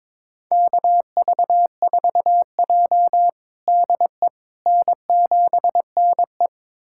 Morse code, heard as a keyed tone, about 22 wpm, 700 Hz. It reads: KV4J DE N7NE
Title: USA Callsign DE Callsign - 4 Characters - 22wpm